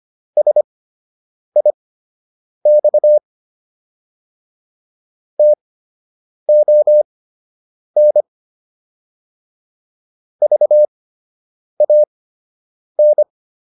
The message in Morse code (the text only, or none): SIX TON VAN